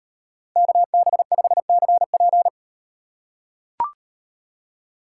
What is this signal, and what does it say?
Morse code, heard as a keyed tone, about 38 wpm, 700 Hz. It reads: KB5CP